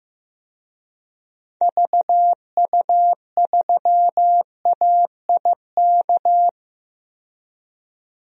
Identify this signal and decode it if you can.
Morse code, heard as a keyed tone, about 15 wpm, 700 Hz. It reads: VU3AIK
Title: DXCC-100 Call Signs - 6 Characters - 15wpm